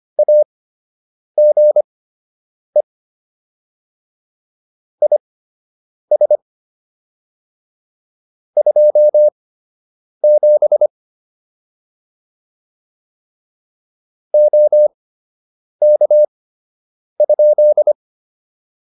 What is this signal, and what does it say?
Morse code, heard as a keyed tone, about 25 wpm, spaced as for 8 wpm, 600 Hz. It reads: AGE IS 27 OK?